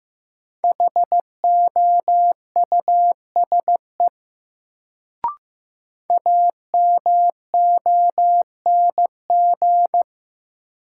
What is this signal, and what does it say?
Morse code, heard as a keyed tone, about 15 wpm, 700 Hz. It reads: HOUSE AMONG
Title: Sets of 2 Words - New Word Drill - 15wpm